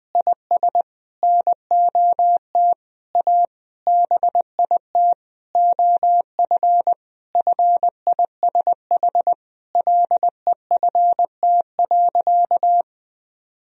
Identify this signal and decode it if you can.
Morse code, heard as a keyed tone, about 20 wpm, 700 Hz. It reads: IS NOT A BIT OF FISH LEFT.
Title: Sentences from Top 1000 Words - Review Part 1 - 20wpm